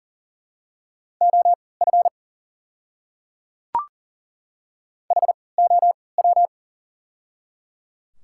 Morse code, heard as a keyed tone, about 40 wpm, spaced as for 23 wpm, 700 Hz. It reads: OF HOW
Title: 3x Character Spacing - Top 100 Words - 40wpm